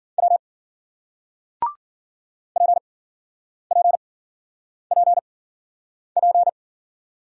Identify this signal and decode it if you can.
Morse code, heard as a keyed tone, about 56 wpm, spaced as for 9 wpm, 700 Hz. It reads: W PPPP